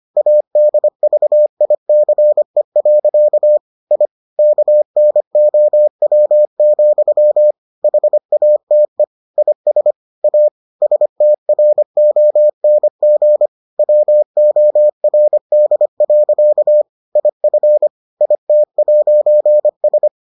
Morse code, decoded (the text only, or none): ADVICE. I KNOW, HATE IS A STRONG WORD. IF IT'S